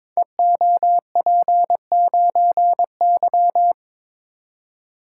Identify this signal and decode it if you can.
Morse code, heard as a keyed tone, about 22 wpm, 690 Hz. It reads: EOP9Y
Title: Code Groups - 5 Letter:Numbers - 22wpm